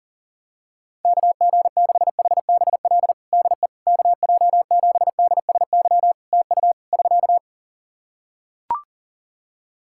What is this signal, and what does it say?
Morse code, heard as a keyed tone, about 37 wpm, 700 Hz. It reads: KG6HBL DE KJ7DSY TU <SK>
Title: K1USN SST QSO Practice 40wpm